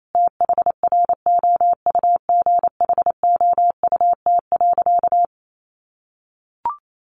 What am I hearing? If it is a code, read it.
Morse code, heard as a keyed tone, about 28 wpm, 700 Hz. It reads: THROUGHOUT.
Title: Sentences from Top 800 Words 28wpm